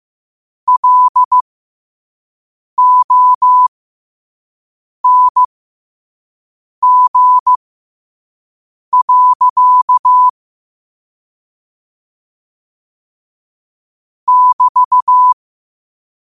Morse code, heard as a keyed tone, about 15 wpm, spaced as for 5 wpm, 1000 Hz. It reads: LONG. <BT>
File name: qotdfiles / 5wpm